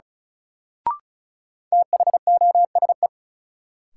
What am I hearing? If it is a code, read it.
Morse code, heard as a keyed tone, about 36 wpm, 700 Hz. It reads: THOSE
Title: Sets of 2 Words - New Word Drill - 35wpm